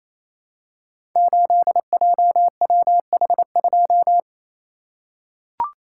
Morse code, heard as a keyed tone, about 27 wpm, 700 Hz. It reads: 8JWH2
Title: Code Groups - 5 Letter:Numbers - 28wpm